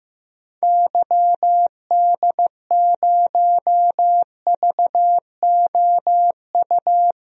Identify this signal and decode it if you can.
Morse code, heard as a keyed tone, about 15 wpm, 700 Hz. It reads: YD0VOU
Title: DXCC-100 Call Signs - 6 Characters - Repeated 3x - 15wpm